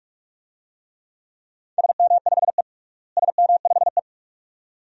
Morse code, heard as a keyed tone, about 46 wpm, 700 Hz. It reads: SM5E SM5E
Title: DXCC-100 Call Signs - 4 Characters - Repeated 2x - 45wpm